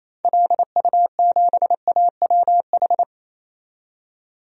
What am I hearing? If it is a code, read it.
Morse code, heard as a keyed tone, about 28 wpm, 700 Hz. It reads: LU7AWH
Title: DXCC-100 Call Signs - 6 Characters - 28wpm